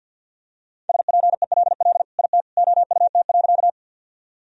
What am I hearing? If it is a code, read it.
Morse code, heard as a keyed tone, about 47 wpm, 700 Hz. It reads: SPELL IT OUT.
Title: Sentences from Top 1000 Words - Review Part 1 - 50wpm